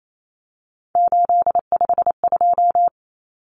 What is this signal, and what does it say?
Morse code, heard as a keyed tone, about 28 wpm, 700 Hz. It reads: 852
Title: Common Cut Numbers - 3-Digit 28wpm